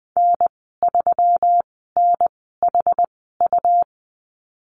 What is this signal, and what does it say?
Morse code, heard as a keyed tone, about 20 wpm, spaced as for 15 wpm, 700 Hz. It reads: N3NHU
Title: 231 Lesson 37g - Letter x - Call Signs 20wpm